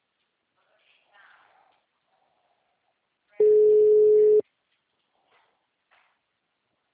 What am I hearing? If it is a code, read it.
Ringback tone (Europe, 425 Hz), with speech.